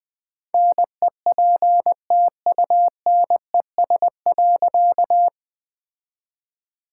Morse code, heard as a keyed tone, about 20 wpm, 700 Hz. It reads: NEPTUNES.
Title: All-Age Appropriate Jokes - 20wpm